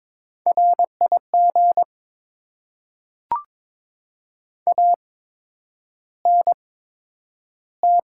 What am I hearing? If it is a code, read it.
Morse code, heard as a keyed tone, about 22 wpm, spaced as for 9 wpm, 700 Hz. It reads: RIG ANT
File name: Top 20 QSO Elements - 8x Character Spacing - 22wpm